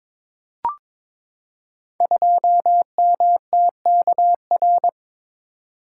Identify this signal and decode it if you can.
Morse code, heard as a keyed tone, about 22 wpm, 700 Hz. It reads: 2MTKR
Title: Code Groups - 5 Letter:Numbers - 22wpm